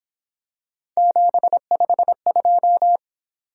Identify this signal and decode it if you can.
Morse code, heard as a keyed tone, about 27 wpm, 700 Hz. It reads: 752
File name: Speed-Racing - Three Letter-Number Combinations - 22wpm